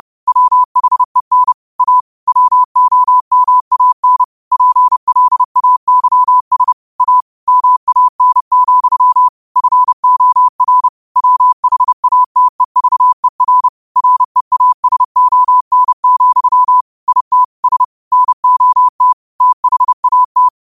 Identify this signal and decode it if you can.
Morse code, heard as a keyed tone, about 30 wpm, 1000 Hz. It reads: WHEN A WOMAN PLAYS A MAN, FOR WHATEVER REASON, IT S NOT THAT
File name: qotdfiles / 30wpm